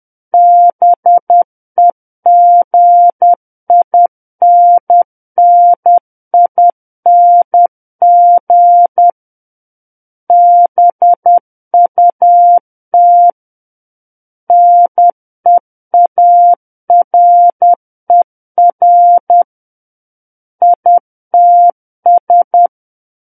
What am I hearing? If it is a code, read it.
Morse code, heard as a keyed tone, about 10 wpm, 700 Hz. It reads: BEGINNING BUT NEARER ITS
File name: Book One / 01-Chapter 1 - 10 WPM